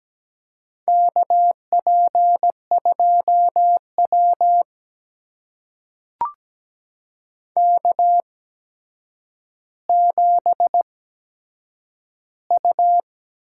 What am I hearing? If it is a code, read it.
Morse code, heard as a keyed tone, about 17 wpm, spaced as for 8 wpm, 700 Hz. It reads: KP2W K7U
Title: USA Call Signs - 4 Characters - 8x Character Spacing - 17wpm